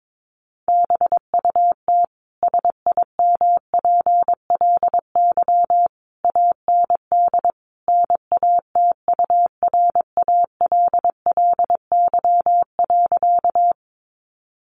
Morse code, heard as a keyed tone, about 22 wpm, 700 Hz. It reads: BUT SIMPLY AND NATURALLY.
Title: Sentences from Top 1300 Words 22wpm